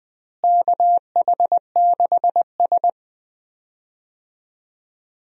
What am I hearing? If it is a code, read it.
Morse code, heard as a keyed tone, about 19 wpm, 700 Hz. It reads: KH6S
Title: DXCC-100 Call Signs - 4 Characters - Repeated 3x - 20wpm